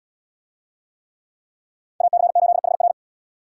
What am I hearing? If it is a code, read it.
Morse code, heard as a keyed tone, about 70 wpm, 700 Hz. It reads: AC8HR